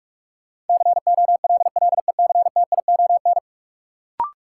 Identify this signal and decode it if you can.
Morse code, heard as a keyed tone, about 44 wpm, 700 Hz. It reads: COLLECTION
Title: Sentences from Top 1200 Words - New Word Drill - 45wpm